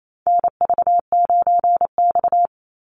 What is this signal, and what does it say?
Morse code, heard as a keyed tone, about 28 wpm, 690 Hz. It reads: NV9X
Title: Call Signs Repeated 3x - 28wpm